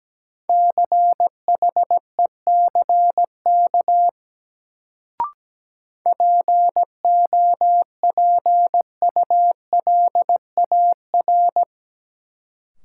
Morse code, heard as a keyed tone, about 17 wpm, 700 Hz. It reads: CHECK POPULAR
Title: Sentences from Top 1000 Words - Review Part 3 - New Word Drill - 17wpm